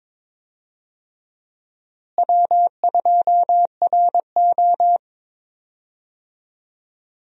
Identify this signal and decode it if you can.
Morse code, heard as a keyed tone, about 22 wpm, 700 Hz. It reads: W2RO
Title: USA Call Signs - 4 Characters - Repeated 3x - 22wpm